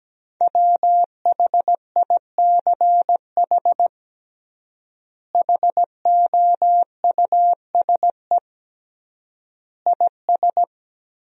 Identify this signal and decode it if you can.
Morse code, heard as a keyed tone, about 17 wpm, 700 Hz. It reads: WHICH HOUSE IS
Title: 3x Word-Spacing - Sentences from Top 200 Words - 17wpm